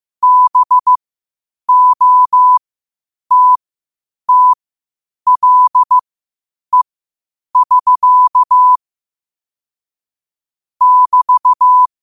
Morse code, heard as a keyed tone, about 15 wpm, spaced as for 8 wpm, 1000 Hz. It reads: BOTTLE<SK> <BT>